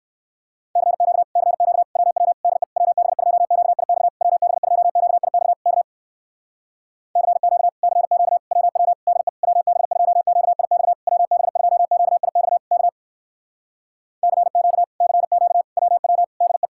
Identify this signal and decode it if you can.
Morse code, heard as a keyed tone, about 59 wpm, 700 Hz. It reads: CQ CQ WW DE WB2QIX WB2QIX K CQ CQ WW DE WB2QIX WB2QIX K CQ CQ WW DE